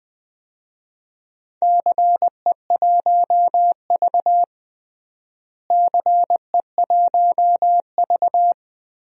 Morse code, heard as a keyed tone, about 21 wpm, 700 Hz. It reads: CE1V CE1V
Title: DXCC-100 Call Signs - 4 Characters - Repeated 2x - 20wpm